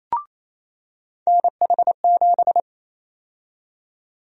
Morse code, heard as a keyed tone, about 28 wpm, 700 Hz. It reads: NH7
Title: USA Call Sign Prefixes - 28wpm